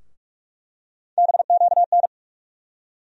Morse code, heard as a keyed tone, about 42 wpm, 700 Hz. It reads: BQN
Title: USA Call Sign Postfixes - 3 Letters 45wpm